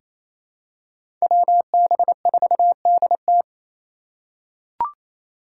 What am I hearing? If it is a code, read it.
Morse code, heard as a keyed tone, about 27 wpm, 700 Hz. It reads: WB4DT
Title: USA Call Signs - 5 Characters - 6x Character Spacing - 28wpm